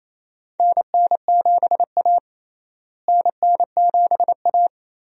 Morse code, heard as a keyed tone, about 28 wpm, 700 Hz. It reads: NN7A NN7A
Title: USA Call Signs - 4 Characters - Repeated 2x - 28wpm